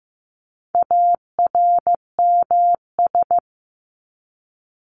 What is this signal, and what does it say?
Morse code, heard as a keyed tone, about 15 wpm, 700 Hz. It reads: ARMS
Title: ES:S Suffix - Top 100 Words with Base Words 15wpm